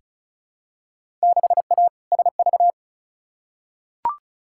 Morse code, heard as a keyed tone, about 34 wpm, 700 Hz. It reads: 6A SV
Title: Field Day Exchanges Encore - 35wpm